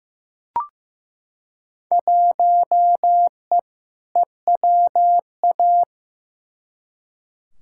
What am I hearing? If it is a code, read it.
Morse code, heard as a keyed tone, about 15 wpm, 700 Hz. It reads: 1E EWA
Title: Field Day Exchanges - 15wpm